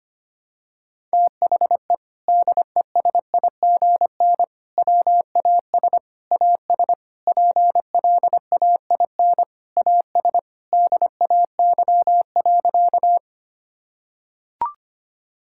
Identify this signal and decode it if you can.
Morse code, heard as a keyed tone, about 25 wpm, 700 Hz. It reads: THE DESIGN WAS AS PLAIN AS DAY.
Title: Sentences from Top 1000 Words - Review Part 5 - 25wpm